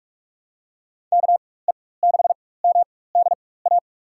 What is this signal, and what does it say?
Morse code, heard as a keyed tone, about 43 wpm, spaced as for 21 wpm, 700 Hz. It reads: KE6MDA
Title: USA Call Signs - 4x Character Spacing - 45wpm